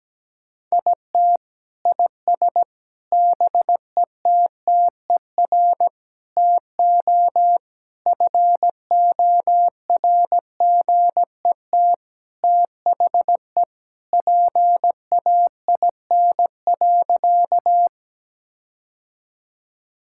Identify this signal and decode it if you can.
Morse code, heard as a keyed tone, about 17 wpm, 700 Hz. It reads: IT IS BETTER TO FORGET THE PAIN.
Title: Sentences from Top 1600 Words 17wpm